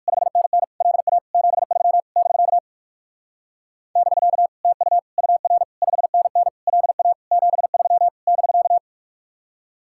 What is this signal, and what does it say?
Morse code, heard as a keyed tone, about 47 wpm, 700 Hz. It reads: 5NN LA 73 <BK> <BK> TU UR 5NN LA 73 <BK>
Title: Speed-Racing Common QSO Phrases - 45wpm